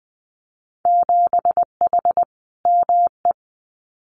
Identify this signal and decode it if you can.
Morse code, heard as a keyed tone, about 20 wpm, 700 Hz. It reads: 7H ME
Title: Winter Field Day Exchanges Encore 20wpm